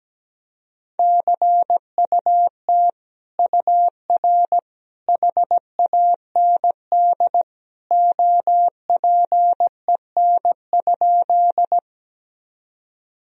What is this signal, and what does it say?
Morse code, heard as a keyed tone, about 17 wpm, 700 Hz. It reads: CUT UR HAND OPEN?
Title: Sentences from Top 300 Words and Abbreviations 17wpm